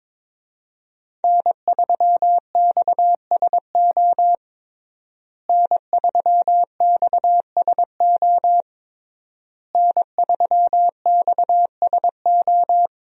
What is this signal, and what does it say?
Morse code, heard as a keyed tone, about 22 wpm, 700 Hz. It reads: N3XSO N3XSO N3XSO
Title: USA Call Signs - 5 Characters - Repeated 3x - 22wpm